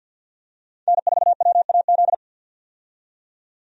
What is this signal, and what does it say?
Morse code, heard as a keyed tone, about 51 wpm, 700 Hz. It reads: N4WAZ
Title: USA Call Signs - 5 Characters - 50wpm